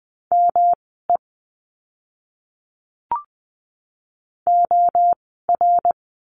Morse code, heard as a keyed tone, about 20 wpm, spaced as for 14 wpm, 700 Hz. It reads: ME OR
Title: 192 Lesson 31f - Letter v - Words 20wpm